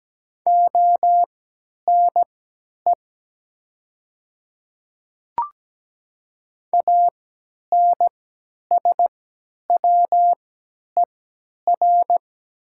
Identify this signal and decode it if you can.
Morse code, heard as a keyed tone, about 17 wpm, spaced as for 10 wpm, 700 Hz. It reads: ONE ANSWER
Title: 3x Character Spacing - Top 300 Words - 17wpm